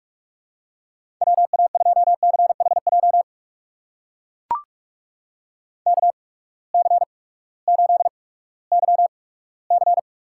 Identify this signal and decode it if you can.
Morse code, heard as a keyed tone, about 45 wpm, spaced as for 20 wpm, 700 Hz. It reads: WA2CHJ KC8YC
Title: USA Call Signs - 6 Characters - 8x Character Spacing - 45wpm